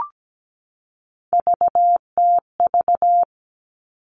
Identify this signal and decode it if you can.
Morse code, heard as a keyed tone, about 17 wpm, 700 Hz. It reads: VTV
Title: All Cut Numbers - 3-Digit 17wpm